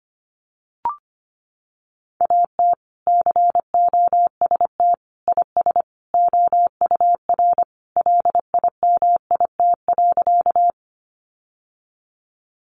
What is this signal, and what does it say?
Morse code, heard as a keyed tone, about 25 wpm, 700 Hz. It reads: AT COST IS OUR LIMIT.
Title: Sentences from Top 1500 Words 25wpm